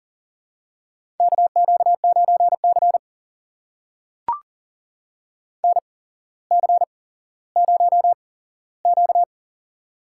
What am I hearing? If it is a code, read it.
Morse code, heard as a keyed tone, about 40 wpm, spaced as for 17 wpm, 700 Hz. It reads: KQ9C NC0Q
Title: USA Call Signs - 4 Characters - 8x Character Spacing - 40wpm